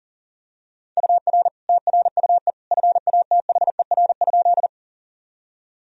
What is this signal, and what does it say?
Morse code, heard as a keyed tone, about 40 wpm, 700 Hz. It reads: UR TRUE FATHER?
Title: Sentences from Top 400 Words and Abbreviations 40wpm